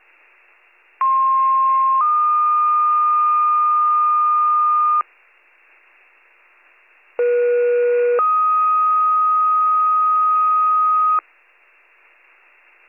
Two-tone page, heard as a one-second tone, then a three-second tone, about 1060 then 1220 Hz, about 480 then 1250 Hz. It is heard from 1.0 to 11.2 s.